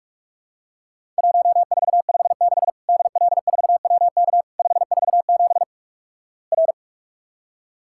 Morse code, heard as a keyed tone, about 48 wpm, 700 Hz. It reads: 1456 DL4WK 547 R
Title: Worked All Europe DX Contest - QTC Exchange - 45wpm